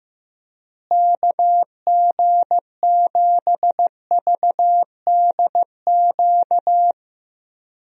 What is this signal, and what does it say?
Morse code, heard as a keyed tone, about 15 wpm, 700 Hz. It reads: KG7VDQ
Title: USA Call Signs - 6 Characters - Repeated 2x - 15wpm